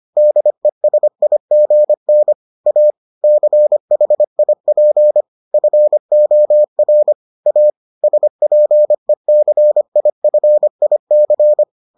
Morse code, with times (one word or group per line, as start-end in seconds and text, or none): 0.2-2.3 DESIGN
2.7-2.9 A
3.2-5.2 CHIP
5.5-7.1 FOR
7.5-7.7 A
8.0-11.6 SPECIFIC